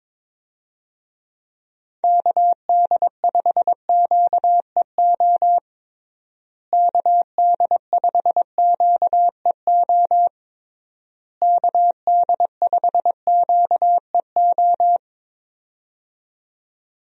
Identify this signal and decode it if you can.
Morse code, heard as a keyed tone, about 22 wpm, 700 Hz. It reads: KD5QEO KD5QEO KD5QEO
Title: USA Call Signs - 6 Characters - Repeated 3x - 22wpm